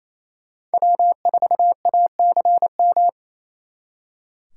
Morse code, heard as a keyed tone, about 34 wpm, 700 Hz. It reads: W4ACM
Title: Speed Racing - USA 1x3 Call Signs 28wpm